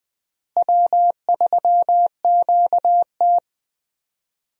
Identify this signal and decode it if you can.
Morse code, heard as a keyed tone, about 20 wpm, 700 Hz. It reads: W3QT
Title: USA Call Signs - 4 Characters - Repeated 2x - 20wpm